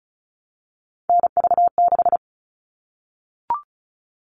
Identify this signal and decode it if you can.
Morse code, heard as a keyed tone, about 34 wpm, 700 Hz. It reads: NV6
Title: All Cut Numbers - 3-Digit 35wpm